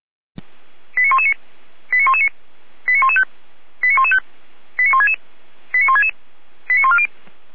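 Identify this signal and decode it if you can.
Five-tone selective call (ZVEI), heard as quick runs of five tones, tones of about 80 ms. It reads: 89119 89119 89115 89115 89166 89166 89140
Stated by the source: ZVEI five-tone selective call 89078 received on SDR.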